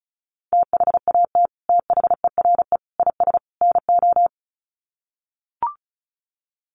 Morse code, heard as a keyed tone, about 34 wpm, 700 Hz. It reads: THAT THERE IS NO
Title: Sets of 4 Words 35wpm